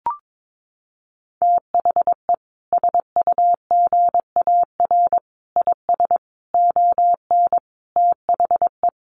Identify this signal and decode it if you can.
Morse code, heard as a keyed tone, about 22 wpm, 700 Hz. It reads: THE SUGAR IS ON THE
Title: Sentences from Top 1200 Words 22wpm